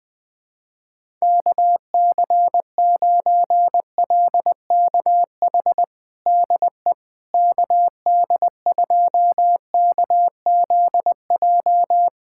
Morse code, heard as a keyed tone, about 20 wpm, 700 Hz. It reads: KC9LKH DE KD2KZJ
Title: USA Callsign DE Callsign 20wpm